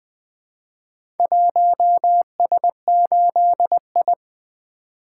Morse code, heard as a keyed tone, about 20 wpm, 700 Hz. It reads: 1S8I
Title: Four Letter-Number Combinations - 20wpm